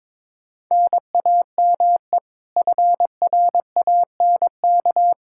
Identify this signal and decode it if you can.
Morse code, heard as a keyed tone, about 22 wpm, 700 Hz. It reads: NAME FRANK
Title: K1USN SST QSO Practice 22wpm